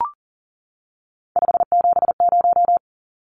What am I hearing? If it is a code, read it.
Morse code, heard as a keyed tone, about 40 wpm, 700 Hz. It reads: 570
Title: Numbers - 3-Digit 40wpm